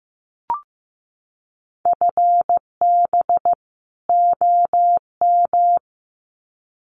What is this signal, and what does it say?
Morse code, heard as a keyed tone, about 15 wpm, 700 Hz. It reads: FB OM
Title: Common QSO Phrases 15wpm